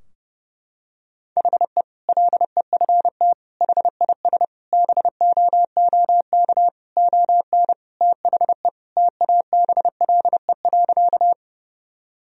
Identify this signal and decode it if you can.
Morse code, heard as a keyed tone, about 30 wpm, 700 Hz. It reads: HE LEFT HIS BOOK ON THE TABLE.
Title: Sentences from Top 1000 Words - Review Part 2 - 30wpm